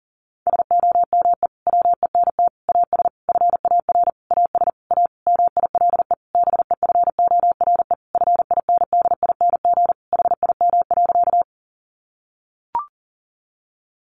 Morse code, heard as a keyed tone, about 40 wpm, 700 Hz. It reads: SOME WENT AS FAR AS A MILE BEFORE FINDING HIM.
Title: Sentences from Top 1100 Words 40wpm